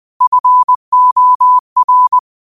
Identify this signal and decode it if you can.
Morse code, heard as a keyed tone, about 20 wpm, 1000 Hz. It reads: FOR